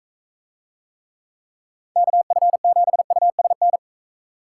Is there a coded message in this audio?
Yes: Morse code from 2.0 to 3.8 s, heard as a keyed tone, about 43 wpm, 700 Hz. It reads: KF7USN